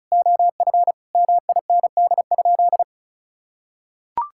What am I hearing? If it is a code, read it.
Morse code, heard as a keyed tone, about 34 wpm, 700 Hz. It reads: OF MIND?
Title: Sentences from Top 1000 Words - Review Part 5 - 35wpm